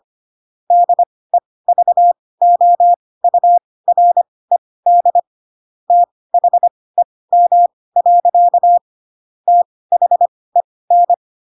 Morse code, heard as a keyed tone, about 25 wpm, spaced as for 18 wpm, 700 Hz. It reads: DEVOURED THEM. THEN
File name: Pinocchio / Pinocchio ch7 12 25 ik1hge